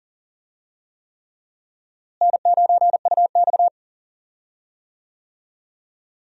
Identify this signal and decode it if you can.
Morse code, heard as a keyed tone, about 40 wpm, 700 Hz. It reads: N9UX
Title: USA Call Signs - 4 Characters - Repeated 3x - 40wpm